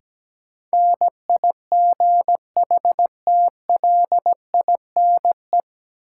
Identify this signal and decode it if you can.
Morse code, heard as a keyed tone, about 17 wpm, 700 Hz. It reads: NIGHTLINE
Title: Sets of 5 Words - New Word Drill - 17wpm